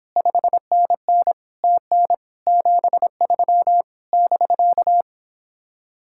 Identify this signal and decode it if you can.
Morse code, heard as a keyed tone, about 29 wpm, 700 Hz. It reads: 5NN TN 73 <BK>